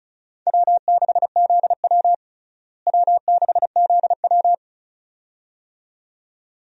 Morse code, heard as a keyed tone, about 36 wpm, 700 Hz. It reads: W6ZW W6ZW
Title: USA Call Signs - 4 Characters - Repeated 2x - 35wpm